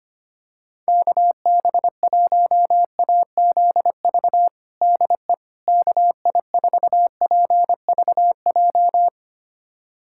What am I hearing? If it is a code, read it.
Morse code, heard as a keyed tone, about 25 wpm, 700 Hz. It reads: KB1AZV DE KI4PVJ
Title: USA Callsign DE Callsign - 6 Characters - 25wpm